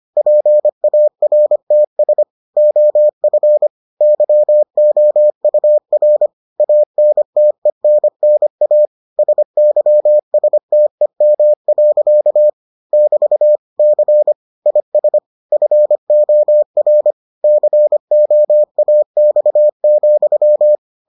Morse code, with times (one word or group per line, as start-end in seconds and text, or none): 0.2-2.2 PARTS
2.6-3.7 OF
4.0-6.3 YOUR
6.6-8.9 ANTENNA
9.2-12.5 SYSTEM.
12.9-13.6 <BT>
13.8-14.3 C
14.7-15.2 IS
15.5-17.1 FOR
17.4-20.8 COAX,